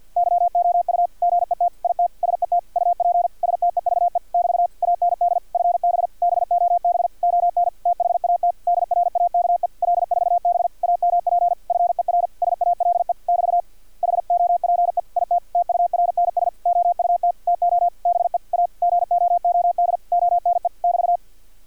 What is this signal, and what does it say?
Morse code, heard as a keyed tone, about 50 wpm, 700 Hz, with radio noise. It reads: YOU GET IT SET UP STEVE <BT> AND FB BOB ON THAT DRAKE L4B AMP FER SALE <BT> HOPE IT TURNS OUT TO BE A GOOD ONE <BT>
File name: archived / CRSnet